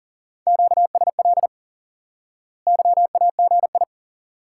Morse code, heard as a keyed tone, about 40 wpm, 700 Hz. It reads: QSL YAGI
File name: IQE Course 3 - Lesson 89 - YAGI - Two Elements - 40wpm